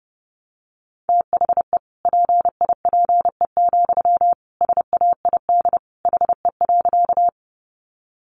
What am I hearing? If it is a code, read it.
Morse code, heard as a keyed tone, about 30 wpm, 700 Hz. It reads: THE PIPE, SAID HE.
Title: Sentences from Top 1500 Words 30wpm